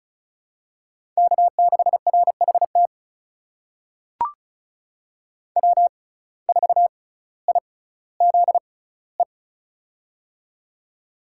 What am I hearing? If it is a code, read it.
Morse code, heard as a keyed tone, about 36 wpm, spaced as for 18 wpm, 700 Hz. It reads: K6RHT W4IZE
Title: USA Call Signs - 5 Characters - 6x Character Spacing - 35wpm